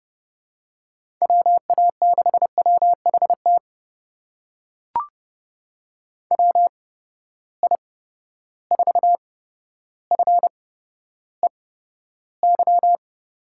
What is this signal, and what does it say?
Morse code, heard as a keyed tone, about 31 wpm, spaced as for 13 wpm, 700 Hz. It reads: WA6WHT WI4FEY
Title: USA Call Signs - 6 Characters - 8x Character Spacing - 30wpm